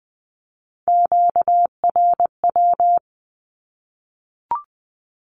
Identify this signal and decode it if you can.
Morse code, heard as a keyed tone, about 20 wpm, 700 Hz. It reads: QRW
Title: Q-Codes 20wpm